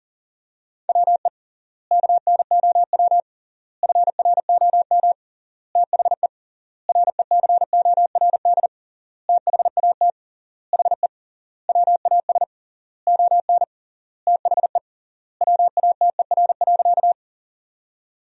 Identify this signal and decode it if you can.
Morse code, heard as a keyed tone, about 40 wpm, 700 Hz. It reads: WE KNOW FROM THE RECORD THAT HE WAS ON THE WATER.
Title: 3x Word-Spacing - Sentences from Top 500 Words - 40wpm